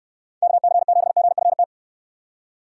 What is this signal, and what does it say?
Morse code, heard as a keyed tone, about 66 wpm, 700 Hz. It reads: KC7GFT